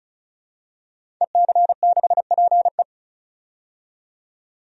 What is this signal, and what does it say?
Morse code, heard as a keyed tone, about 34 wpm, 700 Hz. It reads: EC6PE
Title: DXCC-100 Call Signs - 5 Characters - 35wpm